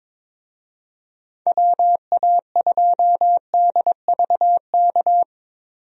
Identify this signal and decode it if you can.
Morse code, heard as a keyed tone, about 22 wpm, 700 Hz. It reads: WA2DVK
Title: USA Call Signs - 6 Characters - 22wpm